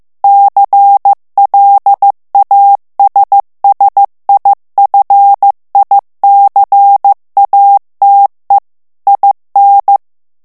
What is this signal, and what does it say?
Morse code, heard as a keyed tone, about 15 wpm, 800 Hz. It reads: CLASSIFICATE IN
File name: morse mp3 / ITALIEN texte clair 2 vitesse 14 mots minute